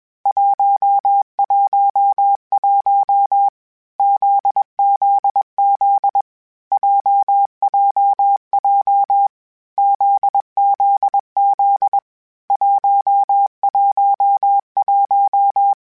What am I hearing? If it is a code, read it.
Morse code, heard as a keyed tone, about 21 wpm, 800 Hz. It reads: 111 ZZZ JJJ ZZZ 111
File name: Basic Sets / Basic Letter Groups Collective7 / JZ17@20